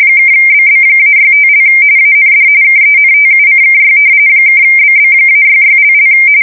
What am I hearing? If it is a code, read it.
RTTY, about 45.45 baud. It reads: (BB-65) AND KENTUCKY (BB-66))
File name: fsk170 / battleshipnj